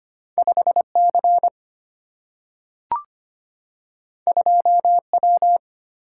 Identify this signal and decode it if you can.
Morse code, heard as a keyed tone, about 25 wpm, 700 Hz. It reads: 5C 2W
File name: Two Letter-Number Combinations - 25wpm